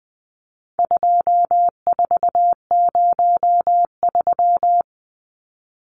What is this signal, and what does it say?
Morse code, heard as a keyed tone, about 20 wpm, 700 Hz. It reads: 2403
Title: Numbers - 4-Digit 20wpm